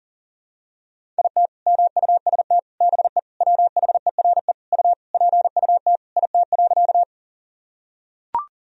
Morse code, heard as a keyed tone, about 40 wpm, 700 Hz. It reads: IT MUST BE WHERE U PUT IT.
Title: Sentences from Top 200 Words and Abbreviations 40wpm